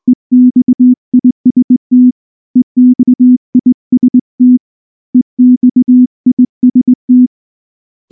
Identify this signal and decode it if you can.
Morse code, heard as a keyed tone, about 19 wpm, 250 Hz. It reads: EXIST EXIST EXIST
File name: morse / morse-training-20WPM-1